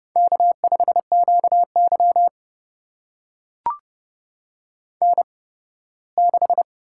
Morse code, heard as a keyed tone, about 30 wpm, spaced as for 18 wpm, 700 Hz. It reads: K5QY N6
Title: USA Call Signs - 8x Character Spacing - 30wpm